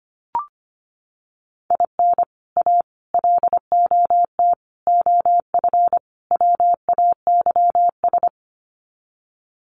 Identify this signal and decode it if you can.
Morse code, heard as a keyed tone, about 25 wpm, 700 Hz. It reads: IN A LOT OF WAYS
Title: Sets of 5 Words Encore 25wpm